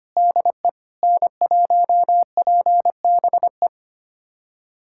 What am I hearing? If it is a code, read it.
Morse code, heard as a keyed tone, about 25 wpm, 700 Hz. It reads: DE N1PBE
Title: USA Callsign DE Callsign - 5 Characters - 25wpm